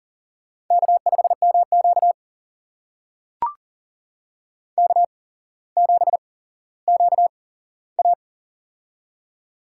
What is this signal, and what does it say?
Morse code, heard as a keyed tone, about 40 wpm, spaced as for 17 wpm, 700 Hz. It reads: K5MQ K7QA
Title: USA Call Signs - 4 Characters - 8x Character Spacing - 40wpm